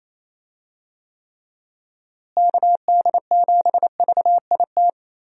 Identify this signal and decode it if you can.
Morse code, heard as a keyed tone, about 28 wpm, 700 Hz. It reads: KD7VIT
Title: USA Call Signs - 6 Characters - Repeated 2x - 28wpm